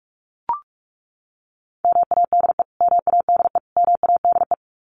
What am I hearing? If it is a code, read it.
Morse code, heard as a keyed tone, about 44 wpm, 700 Hz. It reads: MADE MADE MADE
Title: Top 200 Words Repeated 45wpm